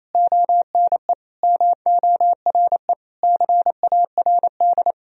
Morse code, heard as a keyed tone, about 28 wpm, 700 Hz. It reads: ONE MORE CARD
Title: Sentences from Top 1000 Words - Review Part 4 - 28wpm